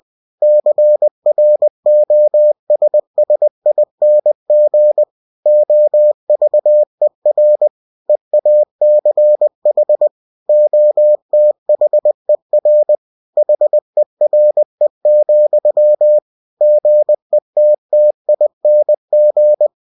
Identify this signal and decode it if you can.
Morse code, heard as a keyed tone, about 20 wpm, 600 Hz. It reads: CROSSING OVER EACH OTHER HERE, GETTING
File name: cw-books / winnie-the-pooh / winnie-20wpm-page-0024